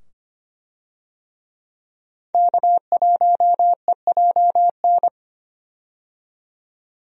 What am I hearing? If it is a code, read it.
Morse code, heard as a keyed tone, about 25 wpm, 700 Hz. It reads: K1EJN
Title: USA Call Signs - 5 Characters - 25wpm